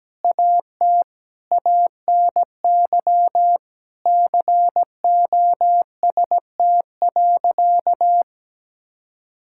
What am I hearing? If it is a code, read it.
Morse code, heard as a keyed tone, about 17 wpm, 700 Hz. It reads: AT ANY COST.
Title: Sentences from Top 1000 Words - Review Part 5 - 17wpm